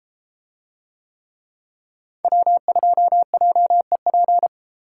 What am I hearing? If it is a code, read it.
Morse code, heard as a keyed tone, about 31 wpm, 700 Hz. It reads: W2JEP